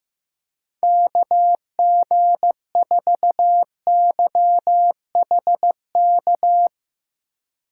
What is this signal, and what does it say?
Morse code, heard as a keyed tone, about 15 wpm, 700 Hz. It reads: KG4YHK
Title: USA Call Signs - 6 Characters - 3x Character Spacing - 15wpm